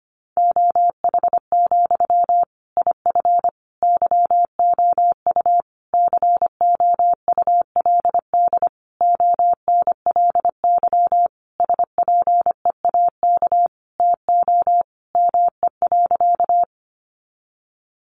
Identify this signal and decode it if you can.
Morse code, heard as a keyed tone, about 25 wpm, 700 Hz. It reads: OH, IF YOU COULD ONLY SPEAK TO ME.
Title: Sentences from Top 1100 Words 25wpm